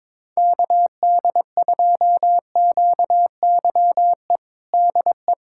Morse code, heard as a keyed tone, about 22 wpm, 700 Hz. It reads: KD2QYE DE
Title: USA Callsign DE Callsign 22wpm